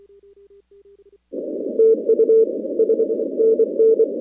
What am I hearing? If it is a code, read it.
Morse code, heard as a keyed tone, about 23 wpm, 450 Hz. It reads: TU S NN
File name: contests / 2019-cw-wpx-kt5j-k5pi- -2330 0-41